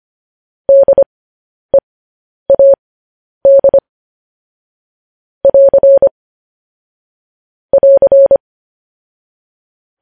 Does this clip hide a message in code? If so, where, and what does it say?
Yes: Morse code from 0.7 to 8.4 s, heard as a keyed tone, about 25 wpm, spaced as for 10 wpm, 550 Hz. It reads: DEAD <AR> <AR>